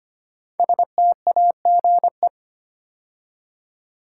Morse code, heard as a keyed tone, about 25 wpm, 700 Hz. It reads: STAGE
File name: 5 Letter Words - First 3 Letters Given - 25wpm